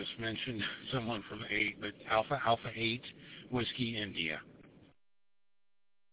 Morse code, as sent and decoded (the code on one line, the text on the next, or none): none